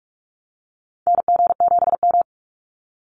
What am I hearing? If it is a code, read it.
Morse code, heard as a keyed tone, about 44 wpm, 700 Hz. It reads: NG7M
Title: Call Signs Repeated 2x - 45wpm